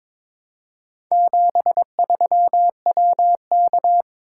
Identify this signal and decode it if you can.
Morse code, heard as a keyed tone, about 22 wpm, 700 Hz. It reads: 73WK
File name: practice / 22wpm / Four Letter-Number Combinations - 22wpm